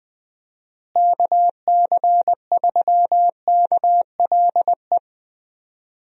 Morse code, heard as a keyed tone, about 20 wpm, 700 Hz. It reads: KC3KLE
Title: USA Call Signs - 6 Characters - Repeated 3x - 20wpm